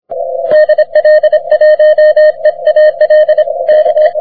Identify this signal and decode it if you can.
Morse code, heard as a keyed tone, about 26 wpm, 610 Hz. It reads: DL1EAL K
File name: wp / vq9jc 30m cw